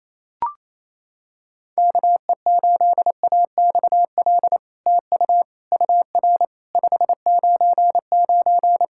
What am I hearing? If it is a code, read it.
Morse code, heard as a keyed tone, about 28 wpm, 700 Hz. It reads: KE8AXL TU UR 599
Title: SOTA QSO Practice 28wpm